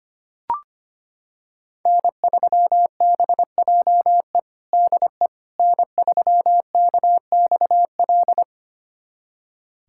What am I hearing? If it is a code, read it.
Morse code, heard as a keyed tone, about 25 wpm, 700 Hz. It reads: N3BJE DE N3KXL
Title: USA Callsign DE Callsign - 5 Characters - 25wpm